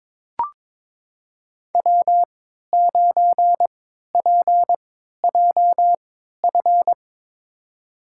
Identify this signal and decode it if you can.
Morse code, heard as a keyed tone, about 22 wpm, spaced as for 13 wpm, 700 Hz. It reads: W9PJF
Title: USA Call Signs - 5 Characters - 3x Character Spacing - 22wpm